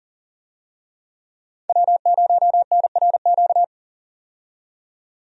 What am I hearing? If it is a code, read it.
Morse code, heard as a keyed tone, about 37 wpm, 700 Hz. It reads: W0NRQ
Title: USA Call Signs - 5 Characters - 40wpm